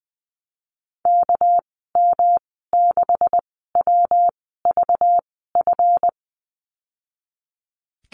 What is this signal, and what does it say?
Morse code, heard as a keyed tone, about 20 wpm, spaced as for 15 wpm, 700 Hz. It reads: KM6WVF
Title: 200 Lesson 32g - Letter k - Call Signs 20wpm